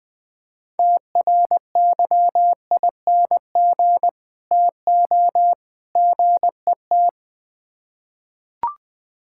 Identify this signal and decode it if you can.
Morse code, heard as a keyed tone, about 21 wpm, 700 Hz. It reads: TRYING TO GET
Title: Build A Phrase - 3-Word Phrases - 20wpm